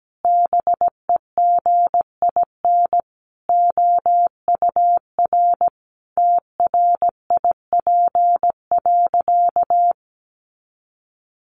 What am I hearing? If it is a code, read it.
Morse code, heard as a keyed tone, about 17 wpm, 700 Hz. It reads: BEGIN OUR TRIP.
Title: Sentences from Top 1300 Words 17wpm